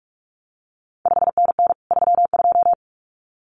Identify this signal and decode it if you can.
Morse code, heard as a keyed tone, about 46 wpm, 700 Hz. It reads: 5NN 32
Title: CQ WW DX Contest Exchange Encore - 45wpm